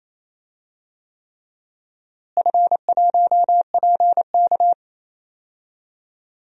Morse code, heard as a keyed tone, about 27 wpm, 700 Hz. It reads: F1PK
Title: DXCC-100 Call Signs - 4 Characters - Repeated 3x - 28wpm